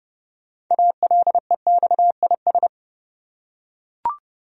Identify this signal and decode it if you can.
Morse code, heard as a keyed tone, about 30 wpm, 700 Hz. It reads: ALEXIS
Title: USA Common Names - 30wpm